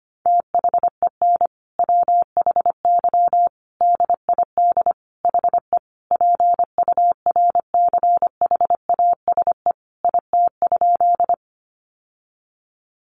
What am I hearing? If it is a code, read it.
Morse code, heard as a keyed tone, about 25 wpm, 700 Hz. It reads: THEN WHY DID HE PURCHASE IT?
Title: Sentences from Top 800 Words 25wpm